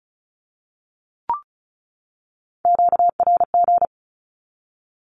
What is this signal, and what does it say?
Morse code, heard as a keyed tone, about 34 wpm, 700 Hz. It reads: QRG
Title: Q-Codes 35wpm